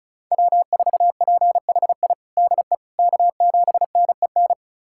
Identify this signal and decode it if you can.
Morse code, heard as a keyed tone, about 36 wpm, 700 Hz. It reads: W4PHI DE K7NEN
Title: USA Callsign DE Callsign - 5 Characters - 35wpm